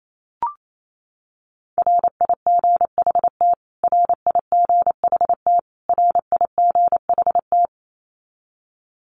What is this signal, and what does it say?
Morse code, heard as a keyed tone, about 28 wpm, 700 Hz. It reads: RIGHT RIGHT RIGHT
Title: Top 300 Words Repeated 28wpm